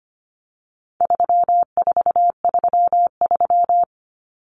Morse code, heard as a keyed tone, about 25 wpm, 700 Hz. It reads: 3433
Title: Numbers - 4-Digit 25wpm